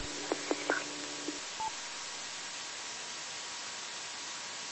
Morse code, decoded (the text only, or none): none